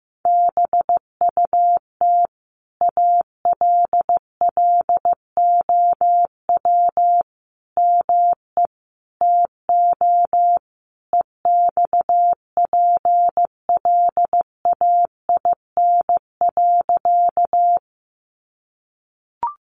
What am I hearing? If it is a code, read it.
Morse code, heard as a keyed tone, about 15 wpm, 700 Hz. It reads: BUT ALLOW ME TO EXPLAIN.
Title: Sentences from Top 1100 Words 15wpm